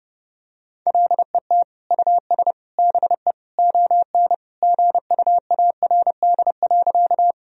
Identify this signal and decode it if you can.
Morse code, heard as a keyed tone, about 30 wpm, 700 Hz. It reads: LET US BE ON GUARD.
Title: Sentences from Top 1000 Words - Review Part 4 - 30wpm